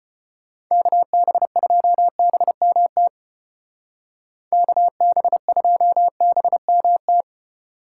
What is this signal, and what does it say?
Morse code, heard as a keyed tone, about 31 wpm, 700 Hz. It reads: KB2BMT KB2BMT
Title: Speed Racing - USA 2x3 Call Signs 25wpm